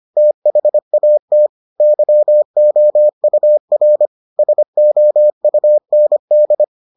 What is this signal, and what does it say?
Morse code, heard as a keyed tone, about 25 wpm, 600 Hz. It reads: THAT YOUR SOUND